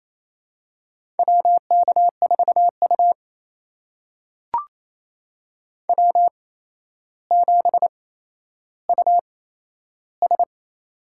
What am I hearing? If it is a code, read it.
Morse code, heard as a keyed tone, about 28 wpm, spaced as for 12 wpm, 700 Hz. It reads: WK4U W7US
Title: USA Call Signs - 4 Characters - 8x Character Spacing - 28wpm